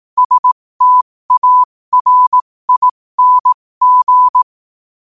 Morse code, heard as a keyed tone, about 18 wpm, spaced as for 15 wpm, 1000 Hz. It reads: STARING
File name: qotdfiles / 15wpm